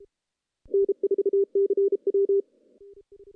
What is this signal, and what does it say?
Morse code, heard as a keyed tone, about 37 wpm, 400 Hz. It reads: N4CW
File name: contests / CW-SS-k5tr- -0100